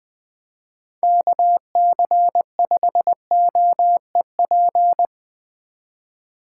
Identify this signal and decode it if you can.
Morse code, heard as a keyed tone, about 19 wpm, 700 Hz. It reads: KC5OEP
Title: Speed Racing - USA 2x3 Call Signs 17wpm